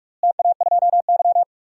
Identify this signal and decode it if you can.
Morse code, heard as a keyed tone, about 47 wpm, 700 Hz. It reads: TA2Y
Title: DXCC-100 Call Signs - 4 Characters - Repeated 2x - 45wpm